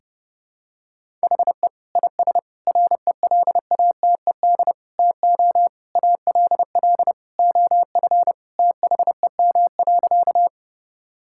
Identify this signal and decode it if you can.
Morse code, heard as a keyed tone, about 30 wpm, 700 Hz. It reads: HE IS RELATED TO ALL OF THEM.
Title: Sentences from Top 1000 Words - Review Part 5 - 30wpm